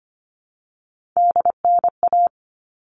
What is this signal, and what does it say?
Morse code, heard as a keyed tone, about 25 wpm, 700 Hz. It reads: DNA
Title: All Cut Numbers - 3-Digit 25wpm